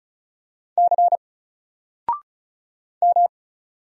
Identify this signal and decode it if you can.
Morse code, heard as a keyed tone, about 35 wpm, 700 Hz. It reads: C M
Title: Single Character (Letter:Number:Punctuation:Prosign) 35wpm